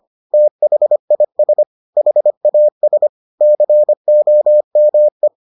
Morse code, heard as a keyed tone, about 25 wpm, 600 Hz. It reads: THIS HAS COME